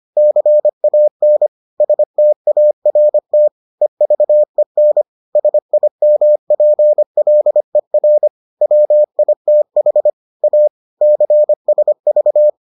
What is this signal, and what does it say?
Morse code, heard as a keyed tone, about 25 wpm, 600 Hz. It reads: CAN START EVEN SIMPLER WITH A CSV